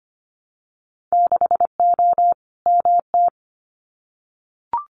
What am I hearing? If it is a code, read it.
Morse code, heard as a keyed tone, about 25 wpm, 700 Hz. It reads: 6O MT
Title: Winter Field Day Exchanges Encore 25wpm